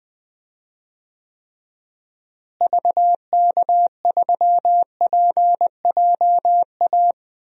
Morse code, heard as a keyed tone, about 20 wpm, 700 Hz. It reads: VK3PJA
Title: DXCC-100 Call Signs - 6 Characters - Repeated 2x - 20wpm